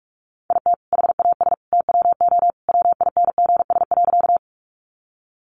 Morse code, heard as a keyed tone, about 44 wpm, 700 Hz. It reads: IT HAS TWO WINGS.
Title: Additional Easy Phrases 45wpm